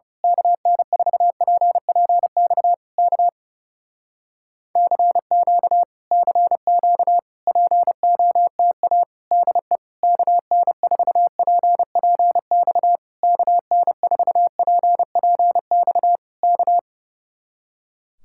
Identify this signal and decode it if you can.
Morse code, heard as a keyed tone, about 30 wpm, 700 Hz. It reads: KN4PPX K CQ CQ POTA DE KN4PPX KN4PPX K
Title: Speed-Racing Common QSO Phrases Encore - 30wpm